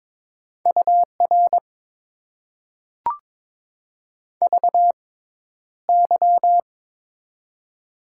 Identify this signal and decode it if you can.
Morse code, heard as a keyed tone, about 22 wpm, spaced as for 11 wpm, 700 Hz. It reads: UR VY
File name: Top 90 QSO Elements - 6x Character Spacing - 22wpm